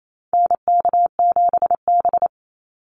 Morse code, heard as a keyed tone, about 28 wpm, 700 Hz. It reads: NK7B
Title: Call Signs Repeated 2x - 28wpm